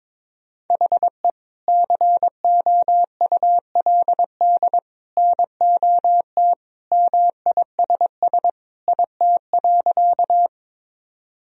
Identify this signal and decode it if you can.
Morse code, heard as a keyed tone, about 22 wpm, 700 Hz. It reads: HE COULD NOT MISS IT.
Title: Sentences from Top 1000 Words - Review Part 3 - 22wpm